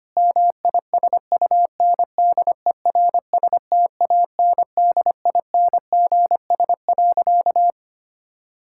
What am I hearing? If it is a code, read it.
Morse code, heard as a keyed tone, about 25 wpm, 700 Hz. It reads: MISUNDERSTANDINGS.
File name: practice / 25wpm / Difficult Sentences with Hints - 25wpm